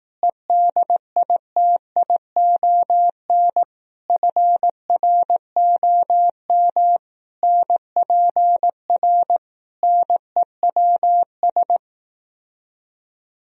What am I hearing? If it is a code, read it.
Morse code, heard as a keyed tone, about 18 wpm, 700 Hz. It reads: EDITION FROM NPR NEWS
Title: Speed-Racing - Sets of 5 Words 15wpm